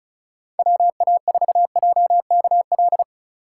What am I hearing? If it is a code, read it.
Morse code, heard as a keyed tone, about 34 wpm, 700 Hz. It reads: WA4JKL
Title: K1USN SST QSO Practice - 35wpm